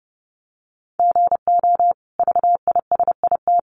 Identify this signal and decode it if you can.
Morse code, heard as a keyed tone, about 30 wpm, 700 Hz. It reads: GO VISIT
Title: Sentences from Top 900 Words 30wpm